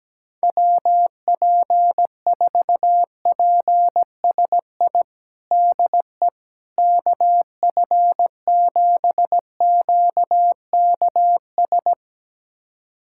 Morse code, heard as a keyed tone, about 17 wpm, 700 Hz. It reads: WP4PSI DE KF7QKS
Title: USA Callsign DE Callsign - 6 Characters - 17wpm